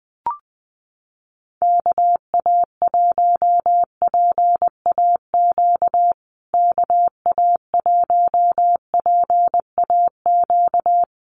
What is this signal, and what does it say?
Morse code, heard as a keyed tone, about 20 wpm, 700 Hz. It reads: KA1PAQ KA1PAQ
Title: USA Call Signs Repeated 3x 20wpm